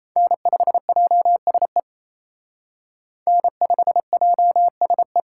Morse code, heard as a keyed tone, about 35 wpm, 700 Hz. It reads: N5JSE N5JSE
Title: Speed Racing - USA 1x3 Call Signs 28wpm